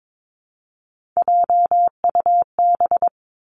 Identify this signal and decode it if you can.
Morse code, heard as a keyed tone, about 22 wpm, 700 Hz. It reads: JUB
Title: Three Challenging Letter-Number Combinations 22wpm